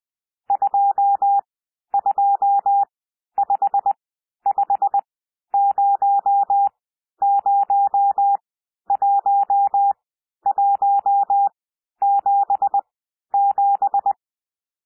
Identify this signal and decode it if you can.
Morse code, heard as a keyed tone, about 20 wpm, spaced as for 11 wpm, 800 Hz. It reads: 2255001177
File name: cwmp3 / N1 SOUND NUMBERS 20 wpm